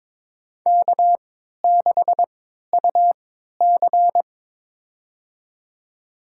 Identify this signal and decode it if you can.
Morse code, heard as a keyed tone, about 22 wpm, spaced as for 13 wpm, 700 Hz. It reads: K6UC
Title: USA Call Signs - 4 Characters - 3x Character Spacing - 22wpm